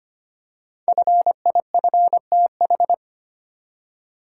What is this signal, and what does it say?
Morse code, heard as a keyed tone, about 25 wpm, 700 Hz. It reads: FIFTH
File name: 5 Letter Words - First 3 Letters Given - 25wpm